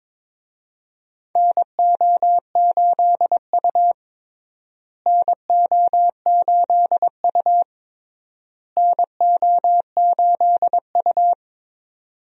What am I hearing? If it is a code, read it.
Morse code, heard as a keyed tone, about 22 wpm, 700 Hz. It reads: NO8U NO8U NO8U
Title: USA Call Signs - 4 Characters - Repeated 3x - 22wpm